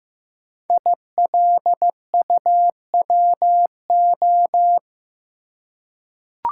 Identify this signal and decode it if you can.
Morse code, heard as a keyed tone, about 15 wpm, 700 Hz. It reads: ILUWO
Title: Code Groups - 5 Letters - 15wpm